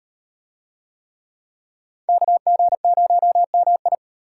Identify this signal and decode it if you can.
Morse code, heard as a keyed tone, about 38 wpm, 700 Hz. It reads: KG0MI